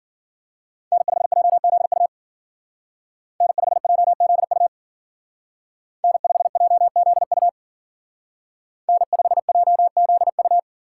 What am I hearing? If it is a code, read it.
Morse code, heard as a keyed tone, about 52 wpm, 700 Hz. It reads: N5JZU N5JZU N5JZU N5JZU